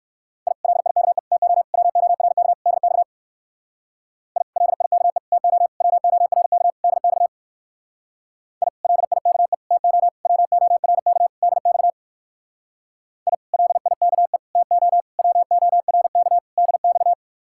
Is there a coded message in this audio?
Yes: Morse code from 0.5 to 17.1 s, heard as a keyed tone, about 61 wpm, 700 Hz. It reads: I LIKE TO WORK DX I LIKE TO WORK DX I LIKE TO WORK DX I LIKE TO WORK DX